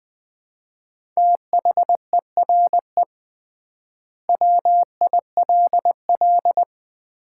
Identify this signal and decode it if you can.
Morse code, heard as a keyed tone, about 20 wpm, 700 Hz. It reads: THERE WILL
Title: 3x Word-Spacing - Sets of 4 Words - 20wpm